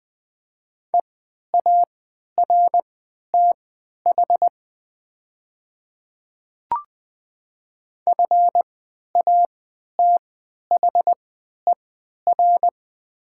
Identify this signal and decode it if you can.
Morse code, heard as a keyed tone, about 20 wpm, spaced as for 11 wpm, 700 Hz. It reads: EARTH FATHER
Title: 3x Character Spacing - Top 200 Words - 20wpm